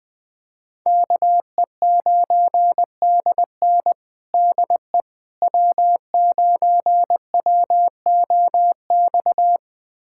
Morse code, heard as a keyed tone, about 20 wpm, 700 Hz. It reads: KE9DN DE W9WOX
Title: USA Callsign DE Callsign - 5 Characters - 20wpm